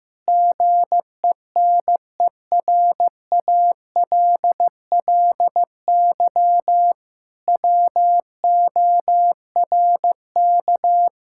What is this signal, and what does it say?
Morse code, heard as a keyed tone, about 15 wpm, 700 Hz. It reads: GENERALLY WORK
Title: Sentences from Top 1000 Words - Review Part 6 - 15wpm